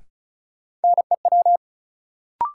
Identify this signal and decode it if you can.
Morse code, heard as a keyed tone, about 36 wpm, 700 Hz. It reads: NEW
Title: Sentences from Top 300 Words - New Word Drill - 35wpm